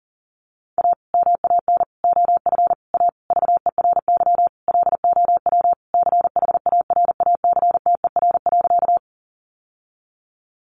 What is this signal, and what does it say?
Morse code, heard as a keyed tone, about 37 wpm, 700 Hz. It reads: A MAN OF A VERY LOW CHARACTER.
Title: Sentences from Top 1200 Words 40wpm